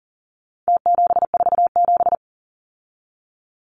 Morse code, heard as a keyed tone, about 40 wpm, 700 Hz. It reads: T747
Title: Numbers - Contest Style - 4-Digit 40wpm